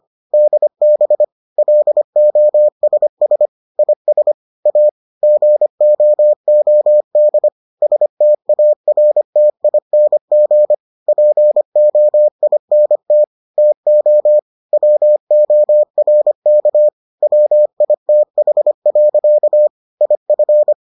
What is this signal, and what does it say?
Morse code, heard as a keyed tone, about 25 wpm, 600 Hz. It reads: DB LOSS IS A GOOD STARTING POINT TO WORK WITH. IF